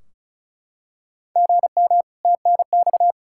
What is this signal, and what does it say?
Morse code, heard as a keyed tone, about 36 wpm, 700 Hz. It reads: GM TNX
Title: Practice QSO - Exchange 1 - Beginning - 35wpm